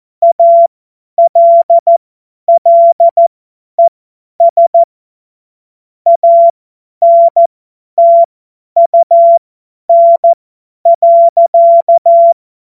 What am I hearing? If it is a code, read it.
Morse code, heard as a keyed tone, about 14 wpm, spaced as for 10 wpm, 680 Hz. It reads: ALLES ANTUN.
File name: Morsekurs / Witze / Witz 02. August 2022 50 F70